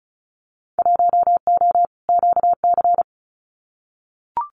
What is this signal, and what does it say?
Morse code, heard as a keyed tone, about 34 wpm, 700 Hz. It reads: 1O QC
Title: Winter Field Day Exchanges 35wpm